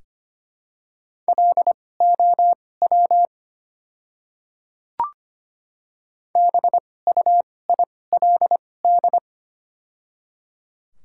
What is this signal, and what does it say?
Morse code, heard as a keyed tone, about 25 wpm, spaced as for 18 wpm, 700 Hz. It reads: LOW BUILD
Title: 2x Character Spacing - Top 300 Words - 25wpm